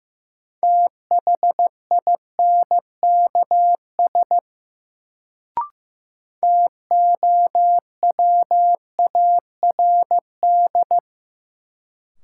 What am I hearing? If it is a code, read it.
Morse code, heard as a keyed tone, about 15 wpm, 700 Hz. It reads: THINKS TOWARD
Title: Sentences from Top 1000 Words - Review Part 6 - New Word Drill - 15wpm